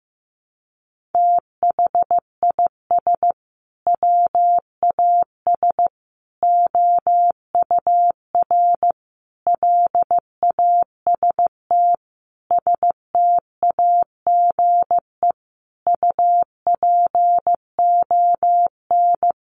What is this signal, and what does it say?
Morse code, heard as a keyed tone, about 15 wpm, 700 Hz. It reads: THIS WAS OUR LAST STAGE UPON
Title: Sentences from Top 1200 Words 15wpm